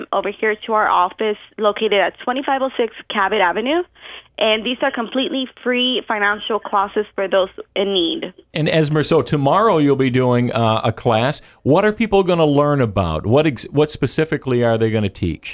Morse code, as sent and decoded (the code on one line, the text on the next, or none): none